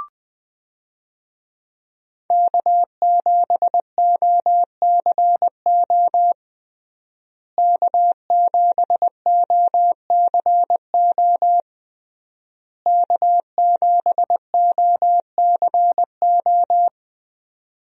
Morse code, heard as a keyed tone, about 20 wpm, 700 Hz. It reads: K7OCO K7OCO K7OCO
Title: USA Call Signs - 5 Characters - Repeated 3x - 20wpm